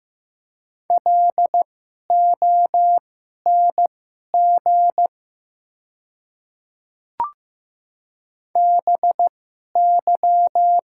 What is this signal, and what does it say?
Morse code, heard as a keyed tone, about 15 wpm, spaced as for 11 wpm, 700 Hz. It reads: LONG BY
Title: 2x Character Spacing - Top 100 Words - 15wpm